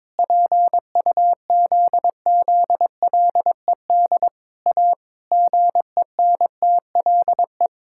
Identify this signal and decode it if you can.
Morse code, heard as a keyed tone, about 22 wpm, 700 Hz. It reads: PUZZLED A GENTLE
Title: Pangram Challenge v2 - 22wpm